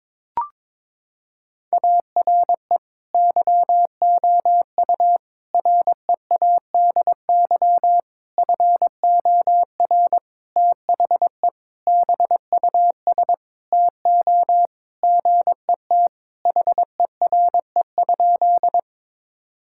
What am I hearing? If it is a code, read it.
Morse code, heard as a keyed tone, about 22 wpm, 700 Hz. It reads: ARE YOU READY FOR THE BUS TO GET HERE?
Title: Sentences from Top 1000 Words - Review Part 6 - 22wpm